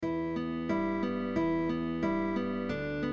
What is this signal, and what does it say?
There is no code.